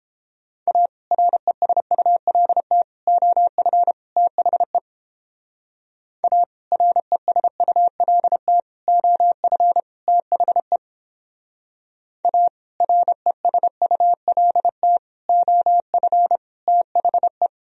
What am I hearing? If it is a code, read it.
Morse code, heard as a keyed tone, about 29 wpm, 700 Hz. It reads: A RESULT OF THE A RESULT OF THE A RESULT OF THE